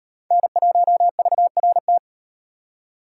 Morse code, heard as a keyed tone, about 37 wpm, 700 Hz. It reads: N1VRT